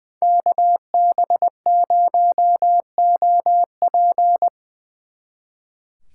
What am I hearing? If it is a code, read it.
Morse code, heard as a keyed tone, about 22 wpm, 700 Hz. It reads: KB0OP
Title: Speed Racing - USA Call Signs 20wpm